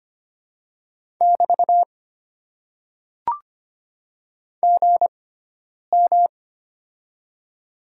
Morse code, heard as a keyed tone, about 25 wpm, spaced as for 9 wpm, 700 Hz. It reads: <BT> GM
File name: Top 90 QSO Elements - 6x Character Spacing - 25wpm